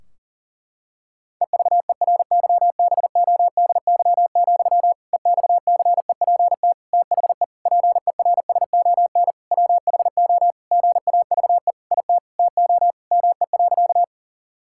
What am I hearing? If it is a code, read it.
Morse code, heard as a keyed tone, about 40 wpm, 700 Hz. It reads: EVERYBODY, EXCEPT THE PERSON WHO GAVE IT TO ME.
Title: Sentences from Top 1600 Words 40wpm